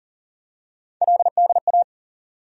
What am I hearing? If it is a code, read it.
Morse code, heard as a keyed tone, about 40 wpm, 700 Hz. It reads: LDA
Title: USA Call Sign Postfixes - 3 Letters 40wpm